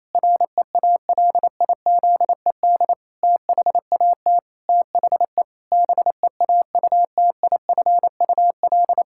Morse code, heard as a keyed tone, about 28 wpm, 700 Hz. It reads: REALIZED THAT THE BEAUTIFUL
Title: Pangram Challenge v2 - 28wpm